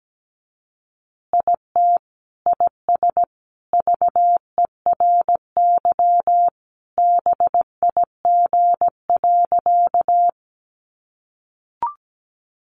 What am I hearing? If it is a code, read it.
Morse code, heard as a keyed tone, about 17 wpm, 700 Hz. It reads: IT IS VERY BIG.
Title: Easy Phrases 17wpm